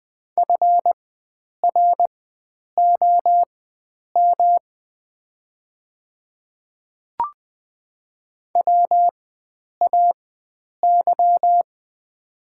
Morse code, heard as a keyed tone, about 20 wpm, spaced as for 9 wpm, 700 Hz. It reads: FROM WAY
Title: 4x Character Spacing - Top 100 Words - 20wpm